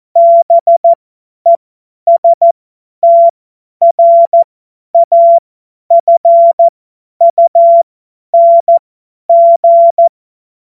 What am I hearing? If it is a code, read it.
Morse code, heard as a keyed tone, about 14 wpm, spaced as for 10 wpm, 680 Hz. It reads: BESTRAFUNG
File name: Morsekurs / Witze / Witz 09. Dezember 2025 50 F70